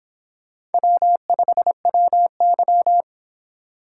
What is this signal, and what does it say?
Morse code, heard as a keyed tone, about 31 wpm, 700 Hz. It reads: W5WY